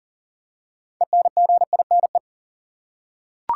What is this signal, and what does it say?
Morse code, heard as a keyed tone, about 40 wpm, 700 Hz. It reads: ENGINE
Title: Six Letter Words - 40wpm